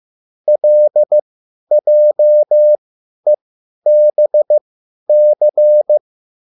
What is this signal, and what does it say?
Morse code, heard as a keyed tone, about 15 wpm, spaced as for 10 wpm, 600 Hz. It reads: LJEBC
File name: QSL / CW lesson